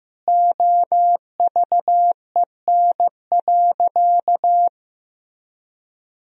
Morse code, heard as a keyed tone, about 15 wpm, 700 Hz. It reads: OVEN.
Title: Sentences from Top 1000 Words - Review Part 4 - 15wpm